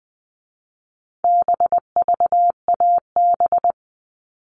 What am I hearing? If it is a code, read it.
Morse code, heard as a keyed tone, about 20 wpm, 700 Hz. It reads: BVAB
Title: All Cut Numbers - 4-Digit 20wpm